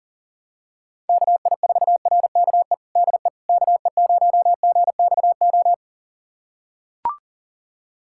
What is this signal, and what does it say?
Morse code, heard as a keyed tone, about 40 wpm, 700 Hz. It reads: KI4RKE DE KE0GXO
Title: USA Callsign DE Callsign - 6 Characters - 40wpm